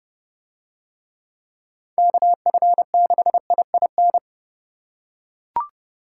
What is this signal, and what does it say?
Morse code, heard as a keyed tone, about 29 wpm, 700 Hz. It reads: KF6IIN